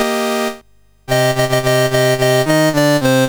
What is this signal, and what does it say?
There is no code.